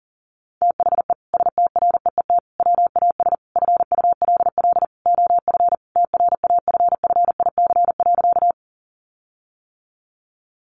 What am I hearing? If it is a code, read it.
Morse code, heard as a keyed tone, about 37 wpm, 700 Hz. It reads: THE STREET WAS FULL OF TRAFFIC.
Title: Sentences from Top 1300 Words 40wpm